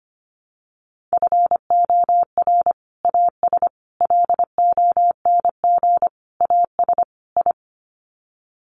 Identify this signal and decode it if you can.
Morse code, heard as a keyed tone, about 25 wpm, 700 Hz. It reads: FOR AS LONG AS I
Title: Sets of 5 Words Encore2 25wpm